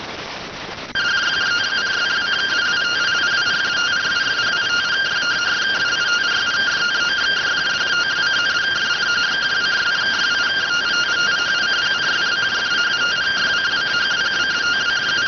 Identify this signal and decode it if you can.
Two-tone teleprinter signal, about 170 Hz shift.